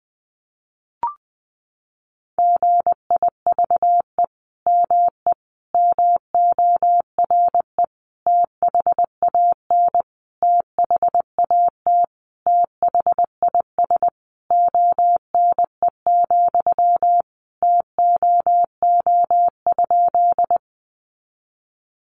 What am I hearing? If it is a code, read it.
Morse code, heard as a keyed tone, about 20 wpm, 700 Hz. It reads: GIVE ME MORE THAN THAT THIS ONE, TOO?
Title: Sentences from Top 200 Words 20wpm